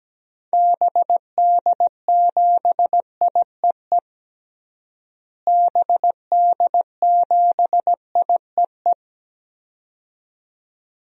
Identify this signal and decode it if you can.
Morse code, heard as a keyed tone, about 17 wpm, 700 Hz. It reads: BD7IEE BD7IEE
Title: DXCC-100 Call Signs - 6 Characters - Repeated 2x - 17wpm